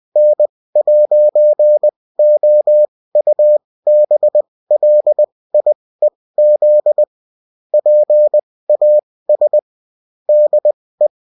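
Morse code, heard as a keyed tone, about 20 wpm, spaced as for 16 wpm, 600 Hz. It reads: N'OUBLIEZ PAS DE
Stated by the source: Bulletin in text and CW form.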